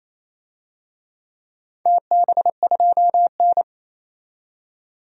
Morse code, heard as a keyed tone, about 28 wpm, 700 Hz. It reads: TB2N
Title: DXCC-100 Call Signs - 4 Characters - 28wpm